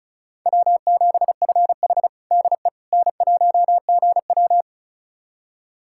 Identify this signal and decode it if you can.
Morse code, heard as a keyed tone, about 36 wpm, 700 Hz. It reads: W7FH DE N1GW
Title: USA Callsign DE Callsign - 4 Characters - 35wpm